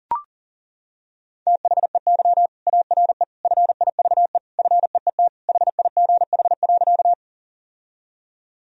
Morse code, heard as a keyed tone, about 40 wpm, 700 Hz. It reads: THEY ARE FIVE FEET HIGH.
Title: Sentences from Top 1000 Words - Review Part 2 - 40wpm